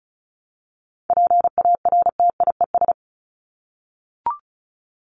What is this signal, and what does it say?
Morse code, heard as a keyed tone, about 34 wpm, 700 Hz. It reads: PARTIES
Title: ES:S Suffix - Top 100 Words with Base Words 35wpm